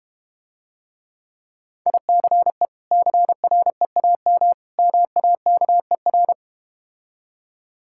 Morse code, heard as a keyed tone, about 32 wpm, 700 Hz. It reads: ICE CREAM MAKER